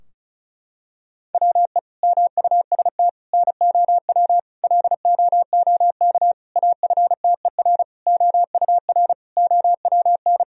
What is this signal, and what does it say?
Morse code, heard as a keyed tone, about 36 wpm, 700 Hz. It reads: WE MUST NOW LOOK AFTER OUR OWN
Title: Sentences from Top 1000 Words - Review Part 4 - 35wpm